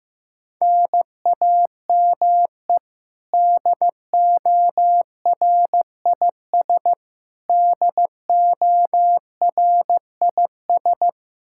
Morse code, heard as a keyed tone, about 15 wpm, 700 Hz. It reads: NAME DORIS DORIS
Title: Practice QSO - Exchange 1 - Beginning - 15wpm